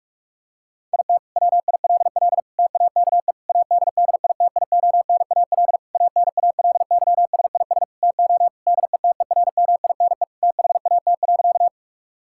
Morse code, heard as a keyed tone, about 46 wpm, 700 Hz. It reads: IT WILL TAKE ADDITIONAL ANALYSIS TO DETERMINE THAT.
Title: Sentences from Top 1000 Words - Review Part 5 - 45wpm